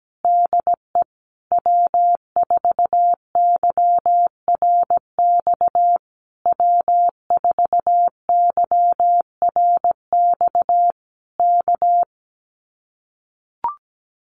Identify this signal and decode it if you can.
Morse code, heard as a keyed tone, about 17 wpm, 700 Hz. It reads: DE W4YRX W4YRX K
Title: Common QSO Phrases Encore 17wpm